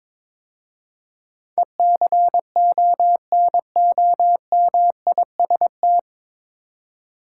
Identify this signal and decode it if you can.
Morse code, heard as a keyed tone, about 22 wpm, 700 Hz. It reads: ECONOMIST
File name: 9 Letter Words - First 3 Letters Given - 22wpm